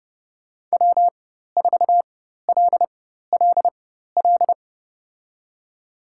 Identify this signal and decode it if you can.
Morse code, heard as a keyed tone, about 30 wpm, spaced as for 14 wpm, 700 Hz. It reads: W4LLL
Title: USA Call Signs - 5 Characters - 4x Character Spacing - 30wpm